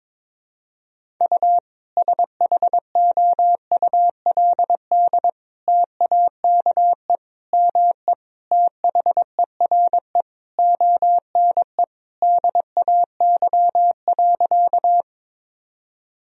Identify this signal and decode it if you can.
Morse code, heard as a keyed tone, about 22 wpm, 700 Hz. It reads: U SHOULD TAKE ME THERE ONE DAY.
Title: Sentences from Top 300 Words and Abbreviations 22wpm